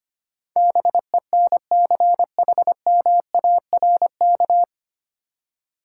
Morse code, heard as a keyed tone, about 25 wpm, 700 Hz. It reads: BENCHMARK
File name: practice / 25wpm / 9 Letter Words - First 3 Letters Given - 25wpm